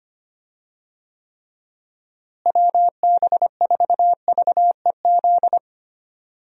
Morse code, heard as a keyed tone, about 25 wpm, 700 Hz. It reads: WB4VEZ
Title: USA Call Signs - 6 Characters - Repeated 2x - 25wpm